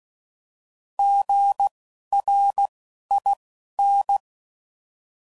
Morse code, heard as a keyed tone, about 16 wpm, spaced as for 11 wpm, 800 Hz. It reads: GRIN
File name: Prefixes / By First Letter